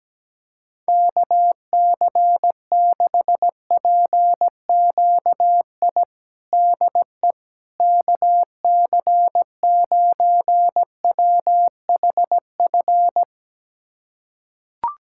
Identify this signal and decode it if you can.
Morse code, heard as a keyed tone, about 17 wpm, 700 Hz. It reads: KC6PQI DE KC9WHF
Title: USA Callsign DE Callsign - 6 Characters - 17wpm